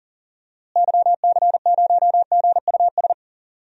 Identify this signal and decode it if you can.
Morse code, heard as a keyed tone, about 40 wpm, 700 Hz. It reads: YC0GUS
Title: DXCC-100 Call Signs - 6 Characters - Repeated 3x - 40wpm